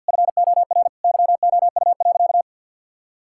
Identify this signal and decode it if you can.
Morse code, heard as a keyed tone, about 46 wpm, 700 Hz. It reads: FOR YOU.
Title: Sentences from Top 1000 Words - Review Part 6 - 50wpm